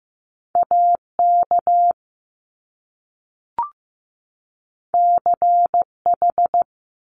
Morse code, heard as a keyed tone, about 15 wpm, 700 Hz. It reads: AK CH
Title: Top Two Letter Combinations in Words 15wpm